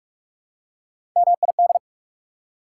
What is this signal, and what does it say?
Morse code, heard as a keyed tone, about 44 wpm, 690 Hz. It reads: MID
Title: Prefixes and Suffixes 45wpm